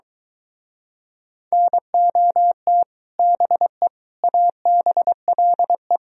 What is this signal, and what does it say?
Morse code, heard as a keyed tone, about 25 wpm, 700 Hz. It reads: NOT BE ABLE